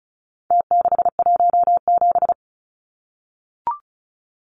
Morse code, heard as a keyed tone, about 34 wpm, 700 Hz. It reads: T617
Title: Numbers - Contest Style - 4-Digit 35wpm